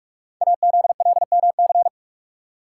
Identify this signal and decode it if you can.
Morse code, heard as a keyed tone, about 44 wpm, 700 Hz. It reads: AZLMC
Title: Code Groups - 5 Letters - 45wpm